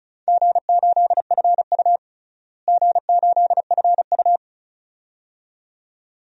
Morse code, heard as a keyed tone, about 34 wpm, 700 Hz. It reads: G8FU G8FU
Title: DXCC-100 Call Signs - 4 Characters - Repeated 2x - 35wpm